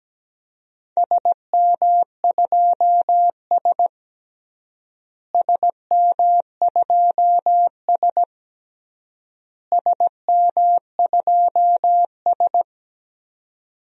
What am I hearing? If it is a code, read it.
Morse code, heard as a keyed tone, about 17 wpm, 700 Hz. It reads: SM2S SM2S SM2S
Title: DXCC-100 Call Signs - 4 Characters - Repeated 3x - 17wpm